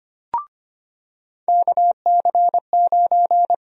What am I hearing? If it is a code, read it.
Morse code, heard as a keyed tone, about 25 wpm, 700 Hz. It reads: KC9
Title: USA Call Sign Prefixes - 25wpm